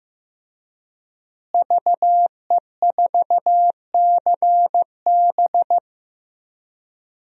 Morse code, heard as a keyed tone, about 15 wpm, 700 Hz. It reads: VE4CB
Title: DXCC-100 Call Signs - 5 Characters - Repeated 3x - 15wpm